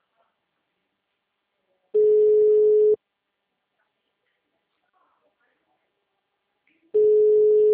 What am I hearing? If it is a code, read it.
Ringback tone (Europe, 425 Hz), with speech.